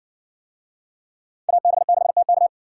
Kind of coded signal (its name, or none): Morse code